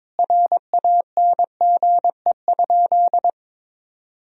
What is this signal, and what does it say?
Morse code, heard as a keyed tone, about 22 wpm, 700 Hz. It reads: RANGE?
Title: Sentences from Top 1000 Words - Review Part 4 - 22wpm